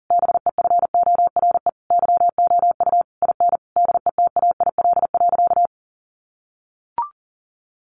Morse code, heard as a keyed tone, about 40 wpm, 700 Hz. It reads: BEFORE YOU IN DETAIL.
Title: Sentences from Top 1500 Words 40wpm